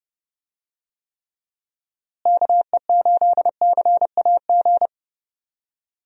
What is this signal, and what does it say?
Morse code, heard as a keyed tone, about 29 wpm, 700 Hz. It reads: KE8CAG
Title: USA Call Signs - 6 Characters - Repeated 2x - 30wpm